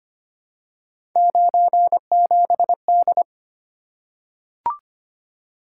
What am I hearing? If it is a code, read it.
Morse code, heard as a keyed tone, about 25 wpm, 700 Hz. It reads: 97D
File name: Three Letter-Number Combinations - 25wpm